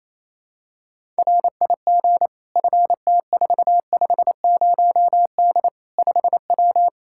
Morse code, heard as a keyed tone, about 27 wpm, 700 Hz. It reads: RIG FT450D 5W
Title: Practice QSO - Exchange 2 - Info - 28wpm